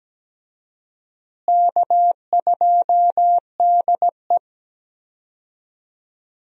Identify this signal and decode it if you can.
Morse code, heard as a keyed tone, about 17 wpm, 700 Hz. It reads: K2DE
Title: USA Call Signs - 4 Characters - Repeated 2x - 17wpm